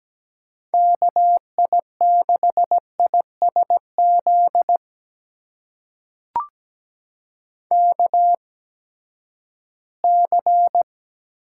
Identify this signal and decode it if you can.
Morse code, heard as a keyed tone, about 17 wpm, spaced as for 12 wpm, 700 Hz. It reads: KI6ISZ KC
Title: USA Call Signs - 6 Characters - 8x Character Spacing - 17wpm